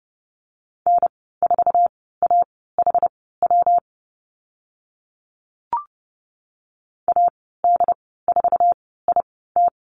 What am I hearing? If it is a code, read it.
Morse code, heard as a keyed tone, about 30 wpm, spaced as for 17 wpm, 700 Hz. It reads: N4AHW AD4IT
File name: Lesson 19g - Letter w - Call Signs 30wpm